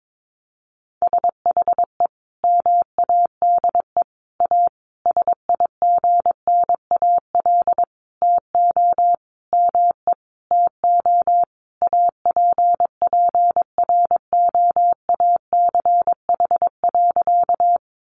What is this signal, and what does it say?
Morse code, heard as a keyed tone, about 22 wpm, 700 Hz. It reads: SHE MADE A SIGNAL TO ME TO APPROACH.
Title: Sentences from Top 1400 Words 22wpm